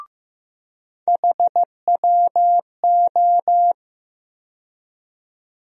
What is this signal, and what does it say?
Morse code, heard as a keyed tone, about 15 wpm, 700 Hz. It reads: HWO
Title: USA Call Sign Postfixes - 3 Letters 15wpm